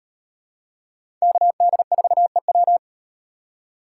Morse code, heard as a keyed tone, about 37 wpm, 700 Hz. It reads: KD4EW